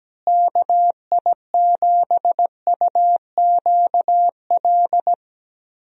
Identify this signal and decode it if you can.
Morse code, heard as a keyed tone, about 17 wpm, 700 Hz. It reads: KI7UQL
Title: USA Call Signs - 6 Characters - Repeated 2x - 17wpm